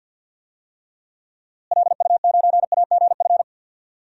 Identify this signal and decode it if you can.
Morse code, heard as a keyed tone, about 52 wpm, 700 Hz. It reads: LU9AGF
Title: DXCC-100 Call Signs - 6 Characters - Repeated 2x - 50wpm